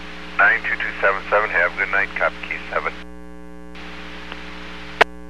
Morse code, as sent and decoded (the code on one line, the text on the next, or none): none